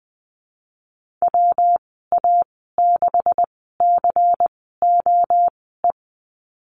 Morse code, heard as a keyed tone, about 19 wpm, spaced as for 14 wpm, 700 Hz. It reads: WA6COE
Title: 173 Lesson 28g - Number 9 - Call Signs 20wpm